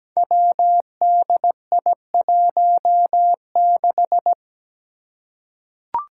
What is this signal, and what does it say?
Morse code, heard as a keyed tone, about 17 wpm, 700 Hz. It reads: WDI16
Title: Five Challenging Letter-Number Combinations 17wpm